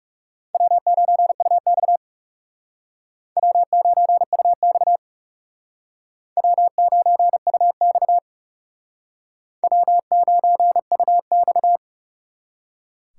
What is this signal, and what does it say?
Morse code, heard as a keyed tone, about 37 wpm, 700 Hz. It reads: W9UX W9UX W9UX W9UX
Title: Speed Racing - USA 1x2 Call Signs 30wpm